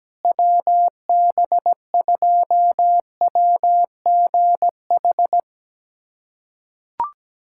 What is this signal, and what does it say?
Morse code, heard as a keyed tone, about 17 wpm, 700 Hz. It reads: WB2WGH
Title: USA Callsign DE Callsign - 6 Characters - 17wpm